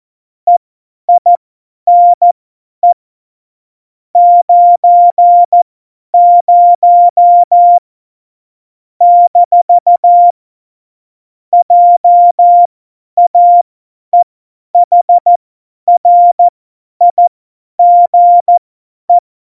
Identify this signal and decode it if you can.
Morse code, heard as a keyed tone, about 14 wpm, spaced as for 10 wpm, 700 Hz. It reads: EINE 90 - JAEHRIGE
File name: Morsekurs / Witze / Witz 20. Oktober 2020, 50 F70